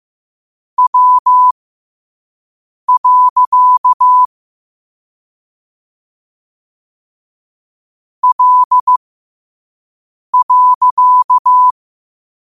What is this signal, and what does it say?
Morse code, heard as a keyed tone, about 15 wpm, spaced as for 5 wpm, 1000 Hz. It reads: W. L.
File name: qotdfiles / 5wpm